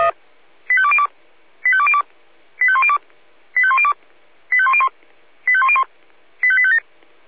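Five-tone selective call (ZVEI), heard as quick runs of five tones, tones of about 70 ms. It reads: 85202 85202 85102 85102 85101 85101 85857
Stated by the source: ZVEI five-tone call 85820 received by SDR.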